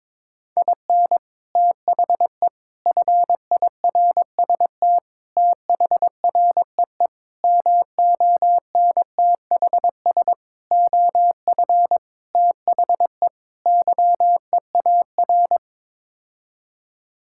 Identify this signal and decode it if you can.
Morse code, heard as a keyed tone, about 22 wpm, 700 Hz. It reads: IN THE FIRST THREE MONTHS OF THE YEAR
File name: practice / 22wpm / Sets of 8 Words - 22wpm